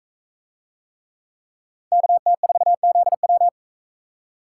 Morse code, heard as a keyed tone, about 39 wpm, 700 Hz. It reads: KT4ZW